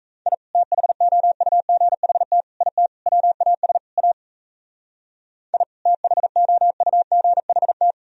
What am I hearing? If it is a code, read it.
Morse code, heard as a keyed tone, about 34 wpm, 700 Hz. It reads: I THOUGHT IT WAS A I THOUGHT
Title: Speed-Racing - Sets of 5 Words 28wpm